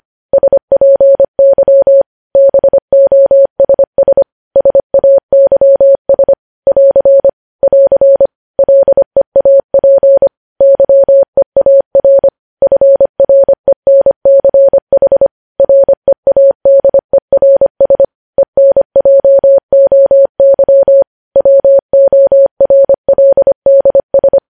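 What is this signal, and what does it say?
Morse code, heard as a keyed tone, about 25 wpm, 550 Hz. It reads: SPY BOSS SAYS <AR> <AR> LEAP YEAR FRENCH READERS ENJOY WORLDS
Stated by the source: Morse news headlines sent at 25 wpm.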